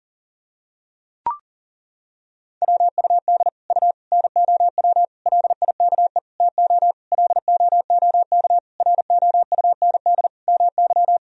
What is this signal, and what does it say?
Morse code, heard as a keyed tone, about 37 wpm, 700 Hz. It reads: WUD U NOW LIKE TO LOOK ROUND MY
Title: Sentences from Top 200 Words and Abbreviations 40wpm